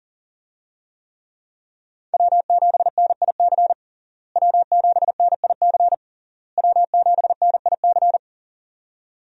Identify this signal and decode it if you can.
Morse code, heard as a keyed tone, about 40 wpm, 700 Hz. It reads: W7NIC W7NIC W7NIC
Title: USA Call Signs - 5 Characters - Repeated 3x - 40wpm